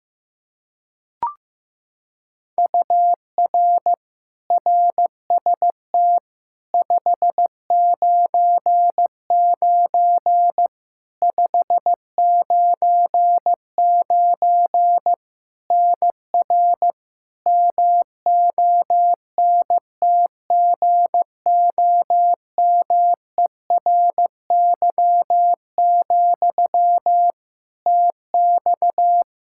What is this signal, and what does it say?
Morse code, heard as a keyed tone, about 16 wpm, 700 Hz. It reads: UR RST 599 599 NR MONTGOMERY, TX
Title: Practice QSO - Exchange 1 - Beginning - 15wpm